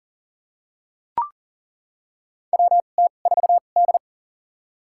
Morse code, heard as a keyed tone, about 37 wpm, spaced as for 28 wpm, 690 Hz. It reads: WT4D
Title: USA Call Signs - 4 Characters - 2x Character Spacing - 40wpm